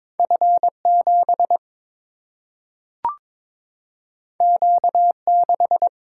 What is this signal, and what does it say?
Morse code, heard as a keyed tone, about 22 wpm, 700 Hz. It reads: F7 Q6
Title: Single-Letter Single-Number 22wpm